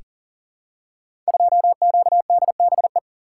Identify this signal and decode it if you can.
Morse code, heard as a keyed tone, about 41 wpm, 690 Hz. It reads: 2QDBE
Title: Code Groups - 5 Letter:Numbers - 40wpm